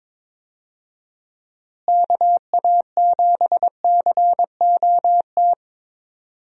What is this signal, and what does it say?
Morse code, heard as a keyed tone, about 23 wpm, 700 Hz. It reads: KA7COT
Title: USA Call Signs - 6 Characters - Repeated 2x - 22wpm